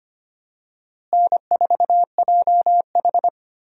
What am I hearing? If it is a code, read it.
Morse code, heard as a keyed tone, about 25 wpm, 700 Hz. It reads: N4JH
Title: USA Call Signs - 4 Characters - Repeated 3x - 25wpm